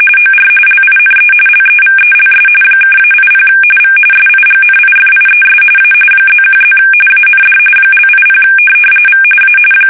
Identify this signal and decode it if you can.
RTTY, about 45.45 baud. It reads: OR LINE LEVELS (-80 V SIGNIFIES A +1+ AND )80 V A
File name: fsk850 / teletype-RTTY